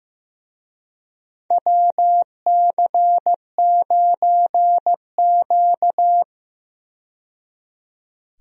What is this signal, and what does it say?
Morse code, heard as a keyed tone, about 15 wpm, 700 Hz. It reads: WC9Q
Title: USA Call Signs - 4 Characters - 15wpm